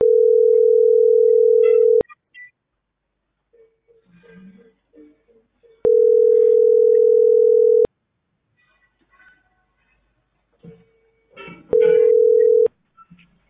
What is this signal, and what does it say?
Ringback tone (North America), 440 + 480 Hz.